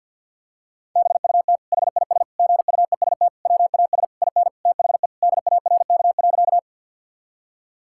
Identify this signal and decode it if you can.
Morse code, heard as a keyed tone, about 50 wpm, 700 Hz. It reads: BUT HIS GUEST WAS IN THE DARK.
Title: Sentences from Top 1600 Words 50wpm